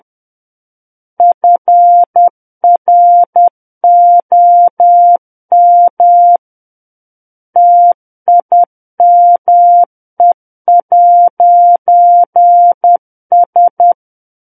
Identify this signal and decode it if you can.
Morse code, heard as a keyed tone, about 10 wpm, 700 Hz. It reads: FROM TIME'S
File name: Book One / 01-Chapter 1 - 10 WPM